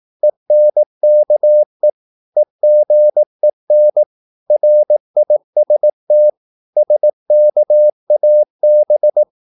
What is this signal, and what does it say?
Morse code, heard as a keyed tone, about 18 wpm, 600 Hz. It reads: ENKE EGEN RIST SKAB
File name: Korte ord DK 18wpm